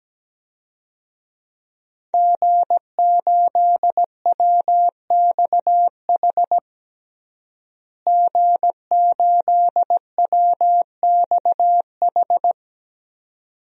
Morse code, heard as a keyed tone, about 17 wpm, 700 Hz. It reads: G8WXH G8WXH
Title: DXCC-100 Call Signs - 5 Characters - Repeated 2x - 17wpm